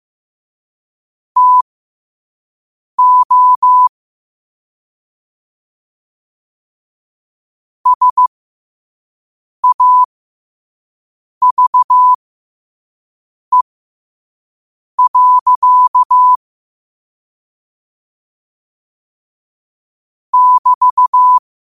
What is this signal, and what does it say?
Morse code, heard as a keyed tone, about 15 wpm, spaced as for 5 wpm, 1000 Hz. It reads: TO SAVE. <BT>
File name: qotdfiles / 5wpm